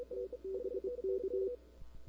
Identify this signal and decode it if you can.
Morse code, heard as a keyed tone, about 25 wpm, 380 Hz. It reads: TBK
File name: CW / 2005-k5tr arrl cw dx -01:00